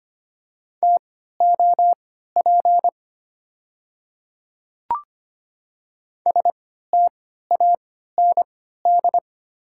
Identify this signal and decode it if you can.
Morse code, heard as a keyed tone, about 25 wpm, spaced as for 14 wpm, 690 Hz. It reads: TOP STAND
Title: 3x Character Spacing - Top 500 Words - 25wpm